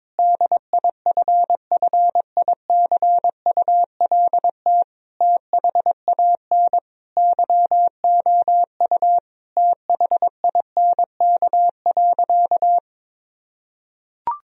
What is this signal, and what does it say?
Morse code, heard as a keyed tone, about 22 wpm, 700 Hz. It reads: DIFFICULT THAN YOU THINK.
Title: Sentences from Top 1000 Words - Review Part 3 - 22wpm